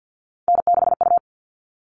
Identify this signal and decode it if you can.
Morse code, heard as a keyed tone, about 50 wpm, 700 Hz. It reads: N6U
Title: All Cut Numbers - 3-Digit 50wpm